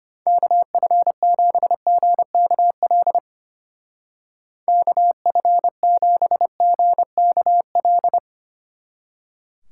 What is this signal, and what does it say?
Morse code, heard as a keyed tone, about 33 wpm, 700 Hz. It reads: KF7GKL KF7GKL